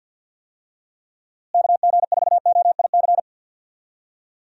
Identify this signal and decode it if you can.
Morse code, heard as a keyed tone, about 51 wpm, 700 Hz. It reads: KG4OIC
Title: USA Call Signs - 6 Characters - Repeated 2x - 50wpm